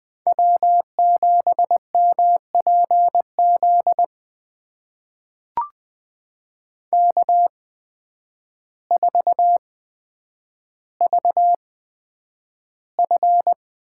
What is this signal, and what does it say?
Morse code, heard as a keyed tone, about 20 wpm, spaced as for 9 wpm, 700 Hz. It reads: W7MPZ K4VF
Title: USA Call Signs - 5 Characters - 8x Character Spacing - 20wpm